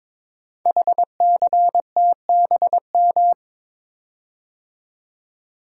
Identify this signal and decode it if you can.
Morse code, heard as a keyed tone, about 22 wpm, 700 Hz. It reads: HCTBM
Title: Code Groups - 5 Letters - 22wpm